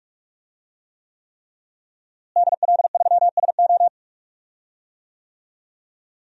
Morse code, heard as a keyed tone, about 43 wpm, 700 Hz. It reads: DL3SO
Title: DXCC-100 Call Signs - 5 Characters - Repeated 3x - 45wpm